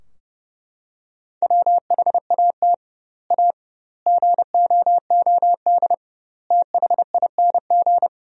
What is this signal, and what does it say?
Morse code, heard as a keyed tone, about 30 wpm, 700 Hz. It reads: WHAT A GOOD THING
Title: 2x Word-Spacing - Sentences from Top 200 Words - 30wpm